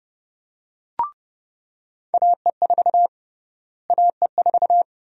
Morse code, heard as a keyed tone, about 32 wpm, 700 Hz. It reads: AE4 AE4
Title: USA Call Sign Prefixes - Repeated 2x - 30wpm